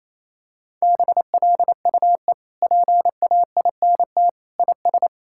Morse code, heard as a keyed tone, about 27 wpm, 700 Hz. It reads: BLUE PAINT IS
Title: Sentences from Top 1000 Words - Review Part 6 - 28wpm